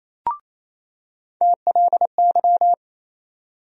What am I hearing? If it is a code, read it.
Morse code, heard as a keyed tone, about 28 wpm, 700 Hz. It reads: TLY
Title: USA Call Sign Postfixes - 3 Letters 28wpm